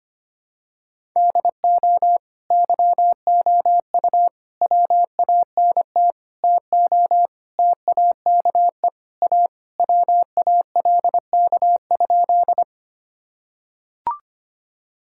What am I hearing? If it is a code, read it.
Morse code, heard as a keyed tone, about 25 wpm, 700 Hz. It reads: DO YOU WANT TO TAKE A WALK?
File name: practice / 25wpm / Sentences with 3-Word Phrases - 25wpm